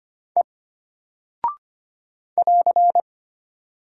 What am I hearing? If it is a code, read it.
Morse code, heard as a keyed tone, about 25 wpm, 700 Hz. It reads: E <AR>
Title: Single Character (Letter:Number:Punctuation:Prosign) Alt 25wpm